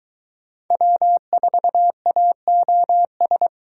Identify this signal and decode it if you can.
Morse code, heard as a keyed tone, about 22 wpm, 700 Hz. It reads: W4AOS